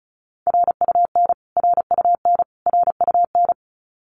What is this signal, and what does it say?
Morse code, heard as a keyed tone, about 34 wpm, 700 Hz. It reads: RUN RUN RUN
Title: Top 500 Words Repeated 35wpm